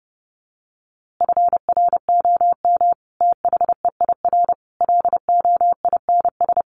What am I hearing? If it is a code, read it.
Morse code, heard as a keyed tone, about 30 wpm, 700 Hz. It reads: FROM THEIR LOINS
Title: A Sound Of Thunder 30wpm